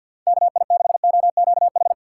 Morse code, heard as a keyed tone, about 46 wpm, 700 Hz. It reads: KI6OQH
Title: USA Call Signs - 6 Characters - 4x Character Spacing - 50wpm